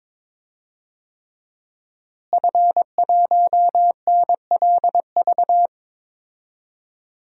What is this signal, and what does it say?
Morse code, heard as a keyed tone, about 22 wpm, 700 Hz. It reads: F1NLV
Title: DXCC-100 Call Signs - 5 Characters - Repeated 3x - 22wpm